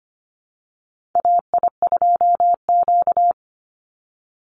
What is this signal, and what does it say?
Morse code, heard as a keyed tone, about 25 wpm, 700 Hz. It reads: AI2Q
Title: Call Signs Repeated 2x - 25wpm